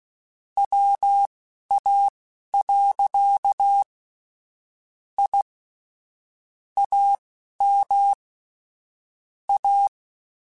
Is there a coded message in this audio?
Yes: Morse code from 0.6 to 9.9 s, heard as a keyed tone, about 16 wpm, spaced as for 11 wpm, 800 Hz. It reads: WA. I AM A